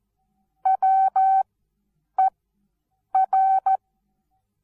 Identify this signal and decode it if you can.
Morse code, heard as a keyed tone, about 14 wpm, spaced as for 8 wpm, 770 Hz. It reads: WER
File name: Onlinelehrgang / morsen / DJ4UF Morsen19